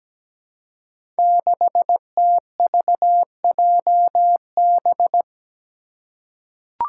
Morse code, heard as a keyed tone, about 17 wpm, 700 Hz. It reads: 6TVJB
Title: Five Challenging Letter-Number Combinations 17wpm